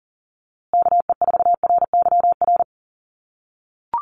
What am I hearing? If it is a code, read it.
Morse code, heard as a keyed tone, about 40 wpm, 700 Hz. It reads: KE4RYR
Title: USA Call Signs Repeated 3x 40wpm